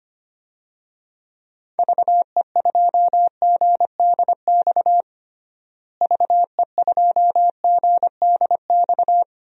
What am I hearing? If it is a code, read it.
Morse code, heard as a keyed tone, about 25 wpm, 700 Hz. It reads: VE2GDX VE2GDX
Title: DXCC-100 Call Signs - 6 Characters - Repeated 3x - 25wpm